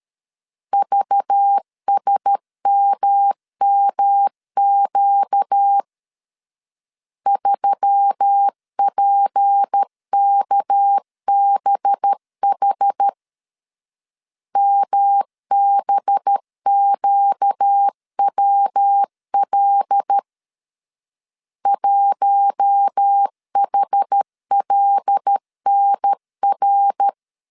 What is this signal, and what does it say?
Morse code, heard as a keyed tone, about 13 wpm, 800 Hz. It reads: VSMMQ 3PKBH MBQWL 1HLNR
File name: CW / DET / 1 6 1500C S B 12wpm 3431